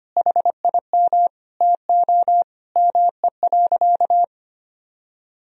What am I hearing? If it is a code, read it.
Morse code, heard as a keyed tone, about 25 wpm, 700 Hz. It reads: HIM TO ME.
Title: Sentences from Top 200 Words and Abbreviations 25wpm